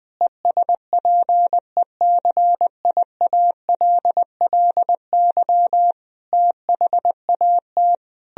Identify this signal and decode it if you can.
Morse code, heard as a keyed tone, about 20 wpm, 700 Hz. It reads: ESPECIALLY THAT
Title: Sentences from Top 1000 Words - Review Part 4 - 20wpm